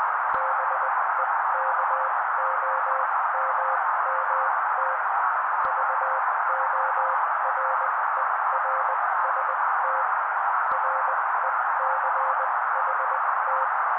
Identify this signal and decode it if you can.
Morse code, heard as a keyed tone, about 20 wpm, spaced as for 16 wpm, 530 Hz. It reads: BEKOMMT VORERST RECHT
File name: daily / 16wpm noise